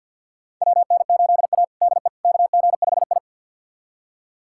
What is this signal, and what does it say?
Morse code, heard as a keyed tone, about 48 wpm, 700 Hz. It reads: WN8A DE KG5I
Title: USA Callsign DE Callsign - 4 Characters - 50wpm